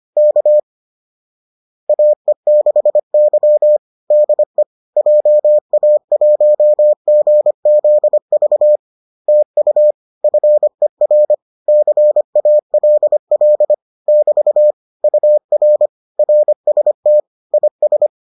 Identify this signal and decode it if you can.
Morse code, heard as a keyed tone, about 25 wpm, 600 Hz. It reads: K AE6Y DE JA1GZV TU FER CALL <BT> UR RST IS